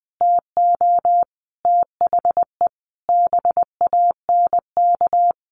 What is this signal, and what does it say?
Morse code, heard as a keyed tone, about 20 wpm, 700 Hz. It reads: TO THE BANK
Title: Sentences from Top 1500 Words 20wpm